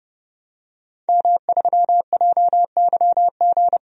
Morse code, heard as a keyed tone, about 32 wpm, 700 Hz. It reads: M3JYG
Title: DXCC-100 Call Signs - 5 Characters - Repeated 3x - 30wpm